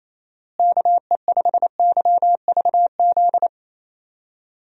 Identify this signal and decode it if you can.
Morse code, heard as a keyed tone, about 28 wpm, 700 Hz. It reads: KE5YVZ
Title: USA Call Signs - 6 Characters - Repeated 3x - 28wpm